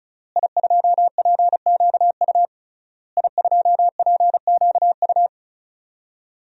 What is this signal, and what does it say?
Morse code, heard as a keyed tone, about 34 wpm, 700 Hz. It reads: I2PQU I2PQU
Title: DXCC-100 Call Signs - 5 Characters - Repeated 3x - 35wpm